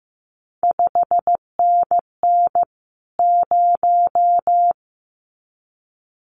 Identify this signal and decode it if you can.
Morse code, heard as a keyed tone, about 15 wpm, 700 Hz. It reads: 5NN 0
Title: CQ WW DX Contest Exchange Encore - 15wpm